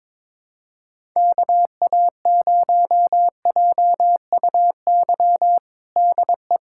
Morse code, heard as a keyed tone, about 22 wpm, 700 Hz. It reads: KA0JUY DE
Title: USA Callsign DE Callsign 22wpm